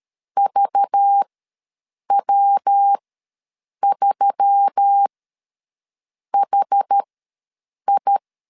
Morse code, heard as a keyed tone, about 13 wpm, spaced as for 7 wpm, 800 Hz. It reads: VW3HI
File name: LEZIONI / 5 LBVC / 5 LBVC ws 12wpm 3159